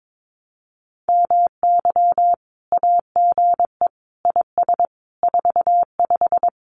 Morse code, heard as a keyed tone, about 22 wpm, 700 Hz. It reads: MY AGE IS 45
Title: Common QSO Phrases Encore 22wpm